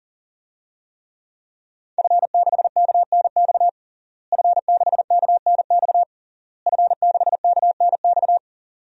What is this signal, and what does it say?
Morse code, heard as a keyed tone, about 40 wpm, 700 Hz. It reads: F6KNX F6KNX F6KNX
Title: DXCC-100 Call Signs - 5 Characters - Repeated 3x - 40wpm